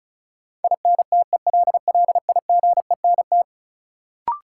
Morse code, heard as a keyed tone, about 36 wpm, 700 Hz. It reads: INTELLIGENT
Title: Sentences from Top 1000 Words - Review Part 7 - New Word Drill - 35wpm